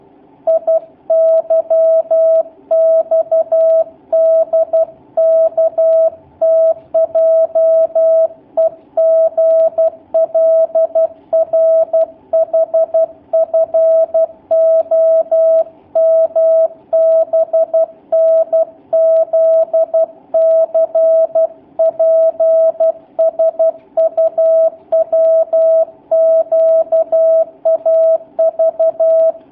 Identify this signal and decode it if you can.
Morse code, heard as a keyed tone, about 12 wpm, 650 Hz. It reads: IYXDKTJEGLRHFOMBNZCPSUWQAV
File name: jh8cbh / 2019cw / sp11- w